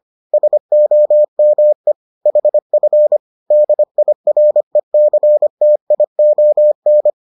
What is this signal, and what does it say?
Morse code, heard as a keyed tone, about 25 wpm, 600 Hz. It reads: SOME HF DIRECTION